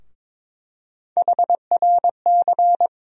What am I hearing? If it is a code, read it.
Morse code, heard as a keyed tone, about 22 wpm, 700 Hz. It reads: HRC
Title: USA Call Sign Postfixes - 3 Letters 22wpm